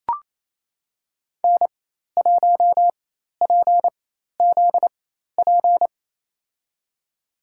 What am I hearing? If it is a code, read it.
Morse code, heard as a keyed tone, about 28 wpm, spaced as for 13 wpm, 700 Hz. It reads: N1PZP
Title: USA Call Signs - 5 Characters - 4x Character Spacing - 28wpm